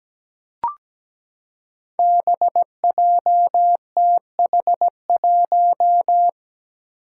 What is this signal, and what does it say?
Morse code, heard as a keyed tone, about 17 wpm, 700 Hz. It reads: BJTH1
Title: Five Challenging Letter-Number Combinations 17wpm